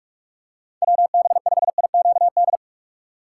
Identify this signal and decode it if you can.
Morse code, heard as a keyed tone, about 46 wpm, 700 Hz. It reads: WB5IQD
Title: USA Call Signs - 6 Characters - 6x Character Spacing - 45wpm